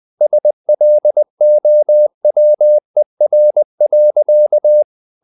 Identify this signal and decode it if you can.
Morse code, heard as a keyed tone, about 20 wpm, 600 Hz. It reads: SLOWER.
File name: the old man and the sea part 2 20wpm